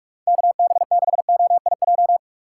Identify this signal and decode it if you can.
Morse code, heard as a keyed tone, about 44 wpm, 700 Hz. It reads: KB6OIJ
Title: USA Call Signs - 6x Character Spacing - 45wpm